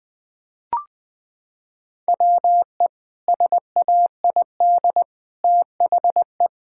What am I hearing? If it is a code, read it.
Morse code, heard as a keyed tone, about 21 wpm, 700 Hz. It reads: WE SAID THE
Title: Sentences with 3-Word Phrases - 20wpm